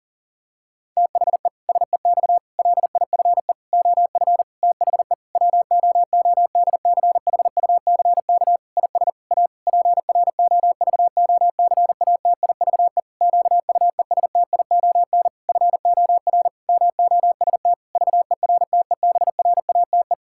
Morse code, heard as a keyed tone, about 40 wpm, 700 Hz. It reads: THE SEX LIFE OF THE WOODCHUCK IS A PROVOCATIVE QUESTION FOR MOST VERTEBRATE
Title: Pangram Challenge v2 - 40wpm